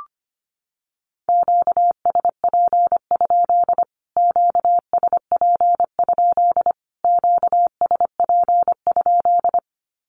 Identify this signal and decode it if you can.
Morse code, heard as a keyed tone, about 25 wpm, 700 Hz. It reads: QSP? QSP? QSP?
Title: Q-Codes Repeated 3x 25wpm